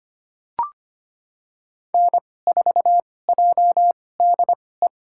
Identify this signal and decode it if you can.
Morse code, heard as a keyed tone, about 25 wpm, spaced as for 18 wpm, 700 Hz. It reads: N4JDE
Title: USA Call Signs - 5 Characters - 2x Character Spacing - 25wpm